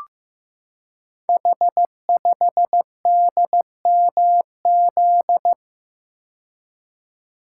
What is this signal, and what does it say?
Morse code, heard as a keyed tone, about 15 wpm, 700 Hz. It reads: H5DMZ
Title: Code Groups - 5 Letter:Numbers - 15wpm